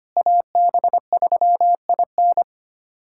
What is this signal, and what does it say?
Morse code, heard as a keyed tone, about 25 wpm, 700 Hz. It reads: AB3IN
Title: USA Callsign DE Callsign - 5 Characters - 25wpm